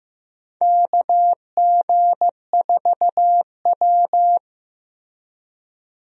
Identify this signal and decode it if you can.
Morse code, heard as a keyed tone, about 16 wpm, 700 Hz. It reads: KG4W
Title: USA Call Signs - 4 Characters - Repeated 3x - 15wpm